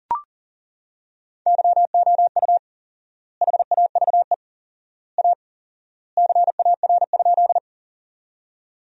Morse code, heard as a keyed tone, about 40 wpm, 700 Hz. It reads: YOU HAVE A CAR?
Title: 4x Word-Spacing - Sentences from Top 300 Words - 40wpm